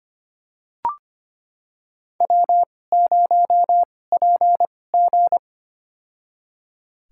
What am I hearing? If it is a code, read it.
Morse code, heard as a keyed tone, about 25 wpm, spaced as for 18 wpm, 700 Hz. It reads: W0PG
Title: USA Call Signs - 2x Character Spacing - 25wpm